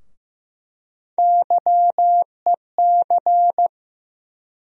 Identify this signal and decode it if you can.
Morse code, heard as a keyed tone, about 15 wpm, 700 Hz. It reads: YEC
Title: USA Call Sign Postfixes - 3 Letters 15wpm